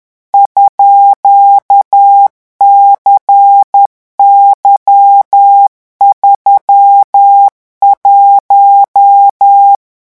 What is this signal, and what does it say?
Morse code, heard as a keyed tone, about 10 wpm, 800 Hz. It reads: _CY31
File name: morse mp3 / revision13-vitesse 10 mots